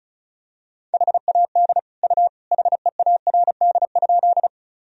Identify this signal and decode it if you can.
Morse code, heard as a keyed tone, about 36 wpm, 700 Hz. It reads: HAD U HEARD?
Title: Sentences from Top 400 Words and Abbreviations 35wpm